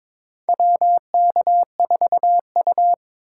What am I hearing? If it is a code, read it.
Morse code, heard as a keyed tone, about 22 wpm, 700 Hz. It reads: WK4U
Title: USA Call Signs - 2x Character Spacing - 22wpm